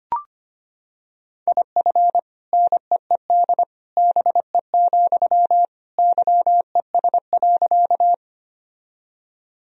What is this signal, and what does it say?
Morse code, heard as a keyed tone, about 25 wpm, 700 Hz. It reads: IF NEED BE, YES.
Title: Sentences from Top 1000 Words - Review Part 1 - 25wpm